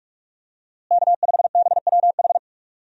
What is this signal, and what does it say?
Morse code, heard as a keyed tone, about 41 wpm, 700 Hz. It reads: K5BWH
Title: USA Call Signs - 5 Characters - Repeated 2x - 45wpm